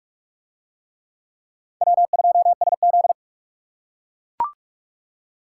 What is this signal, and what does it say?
Morse code, heard as a keyed tone, about 40 wpm, 700 Hz. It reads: W2SZ